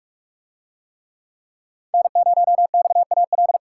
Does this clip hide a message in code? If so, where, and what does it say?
Yes: Morse code from 1.9 to 3.6 s, heard as a keyed tone, about 44 wpm, 700 Hz. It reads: N0XAL